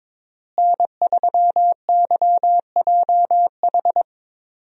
Morse code, heard as a keyed tone, about 22 wpm, 700 Hz. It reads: N3YJH
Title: USA Call Signs - 5 Characters - Repeated 3x - 22wpm